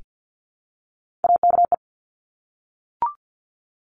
Morse code, heard as a keyed tone, about 50 wpm, 700 Hz. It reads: AKE
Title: Top Three Letter Combinations in Words 50wpm